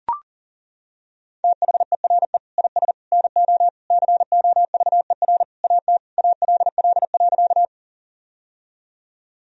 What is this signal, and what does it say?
Morse code, heard as a keyed tone, about 37 wpm, 700 Hz. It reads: THERE IS NO COVER AT ALL.
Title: Sentences from Top 1000 Words - Review Part 1 - 40wpm